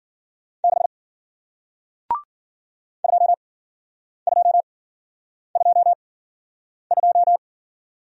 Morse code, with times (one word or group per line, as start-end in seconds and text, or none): 0.6-0.9 6
3.0-7.4 2222